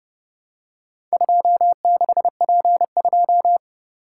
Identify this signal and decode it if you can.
Morse code, heard as a keyed tone, about 29 wpm, 700 Hz. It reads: 26P2